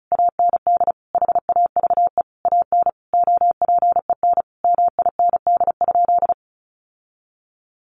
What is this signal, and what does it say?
Morse code, heard as a keyed tone, about 34 wpm, 700 Hz. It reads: AND HAVE AN OPEN MIND?
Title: Sentences from Top 1300 Words 35wpm